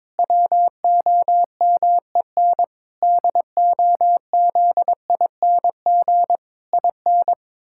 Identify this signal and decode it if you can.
Morse code, heard as a keyed tone, about 21 wpm, 700 Hz. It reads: WOMEN DOZING IN
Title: Pangram Challenge v2 - 22wpm